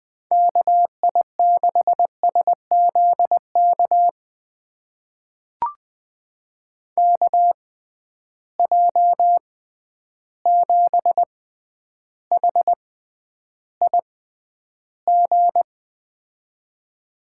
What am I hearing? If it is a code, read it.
Morse code, heard as a keyed tone, about 20 wpm, spaced as for 10 wpm, 700 Hz. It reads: KI6SZK KJ7HIG
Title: USA Call Signs - 6 Characters - 6x Character Spacing - 20wpm